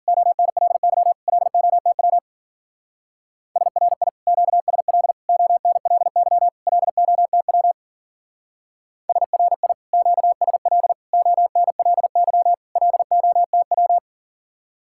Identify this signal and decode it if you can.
Morse code, heard as a keyed tone, about 50 wpm, 700 Hz. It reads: ONLY LOTW SRI QSL ONLY LOTW SRI QSL ONLY LOTW
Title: Speed-Racing Common QSO Phrases - 40wpm